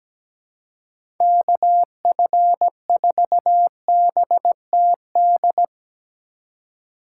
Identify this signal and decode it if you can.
Morse code, heard as a keyed tone, about 17 wpm, 700 Hz. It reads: KF4BTD
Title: USA Call Signs - 2x Character Spacing - 17wpm